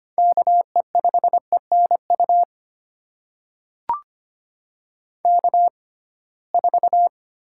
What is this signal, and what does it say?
Morse code, heard as a keyed tone, about 25 wpm, spaced as for 19 wpm, 700 Hz. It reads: KE5ENU K4
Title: USA Call Signs - 6x Character Spacing - 25wpm